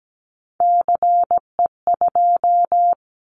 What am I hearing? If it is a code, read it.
Morse code, heard as a keyed tone, about 17 wpm, 700 Hz. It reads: CE2
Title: Callsign Prefixes for Top 100 DXCC - 17wpm